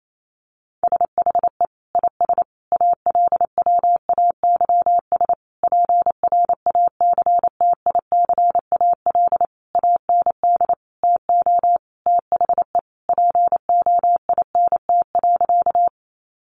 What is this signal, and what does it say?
Morse code, heard as a keyed tone, about 28 wpm, 700 Hz. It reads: SHE IS ALWAYS PRACTICAL AND TO THE POINT.
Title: Sentences from Top 1400 Words 28wpm